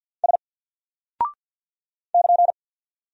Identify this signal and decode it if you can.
Morse code, heard as a keyed tone, about 50 wpm, 700 Hz. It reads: S <KN>
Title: Single Character (Letter:Number:Punctuation:Prosign) 50wpm